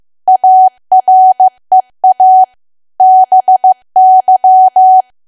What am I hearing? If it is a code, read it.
Morse code, heard as a keyed tone, about 15 wpm, 750 Hz. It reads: AREA BY
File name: cwa / W1AW-Level2 / 15WPM